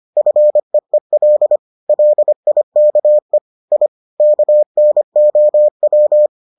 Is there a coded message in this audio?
Yes: Morse code from 0.2 to 6.3 s, heard as a keyed tone, about 25 wpm, 600 Hz. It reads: FEEL LIKE I KNOW